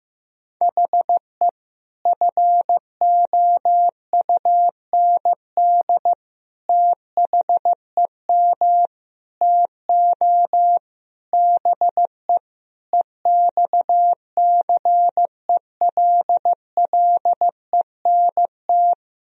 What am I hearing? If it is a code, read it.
Morse code, heard as a keyed tone, about 16 wpm, 700 Hz. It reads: HE FOUND THEM TO BE EXCELLENT
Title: Sentences from Top 1000 Words - Review Part 3 - 15wpm